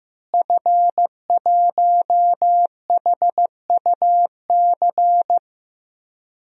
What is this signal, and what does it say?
Morse code, heard as a keyed tone, about 16 wpm, 700 Hz. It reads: F1HUC
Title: DXCC-100 Call Signs - 5 Characters - Repeated 3x - 15wpm